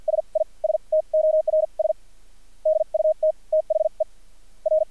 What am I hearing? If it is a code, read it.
Morse code, heard as a keyed tone, about 50 wpm, spaced as for 38 wpm, 630 Hz. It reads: SISTORS DUT THE R
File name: archived / CRSnet